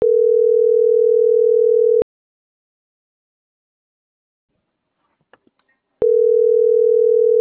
Ringback tone (North America), 440 + 480 Hz.